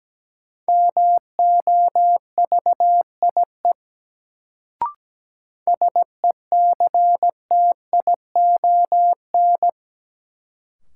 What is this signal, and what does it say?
Morse code, heard as a keyed tone, about 17 wpm, 700 Hz. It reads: MOVIE SECTION
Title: Sentences from Top 1000 Words - Review Part 5 - New Word Drill - 17wpm